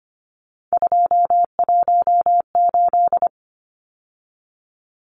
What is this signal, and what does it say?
Morse code, heard as a keyed tone, about 25 wpm, 700 Hz. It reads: 218
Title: Numbers - 3-Digit 25wpm